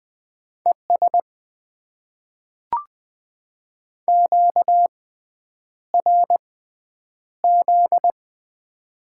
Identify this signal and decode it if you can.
Morse code, heard as a keyed tone, about 20 wpm, spaced as for 9 wpm, 700 Hz. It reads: ES QRZ
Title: Top 90 QSO Elements - 6x Character Spacing - 20wpm